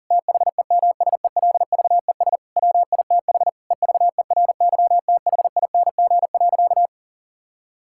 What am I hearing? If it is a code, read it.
Morse code, heard as a keyed tone, about 40 wpm, 700 Hz. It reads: THEMSELVES WITH EVERYTHING.
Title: Sentences from Top 1000 Words - Review Part 5 - 40wpm